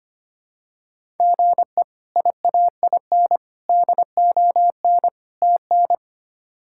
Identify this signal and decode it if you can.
Morse code, heard as a keyed tone, about 25 wpm, 700 Hz. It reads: GE IAIN DON TN
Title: K1USN SST QSO Practice - 25wpm